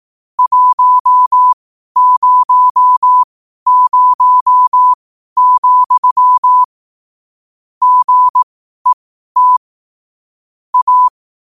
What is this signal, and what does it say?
Morse code, heard as a keyed tone, about 18 wpm, spaced as for 12 wpm, 1000 Hz. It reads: 100, GET A
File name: qotdfiles / 12wpm